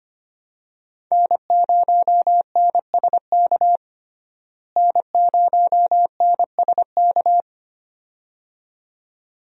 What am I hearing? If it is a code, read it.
Morse code, heard as a keyed tone, about 25 wpm, 700 Hz. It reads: N0NSK N0NSK
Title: USA Call Signs - 5 Characters - Repeated 2x - 25wpm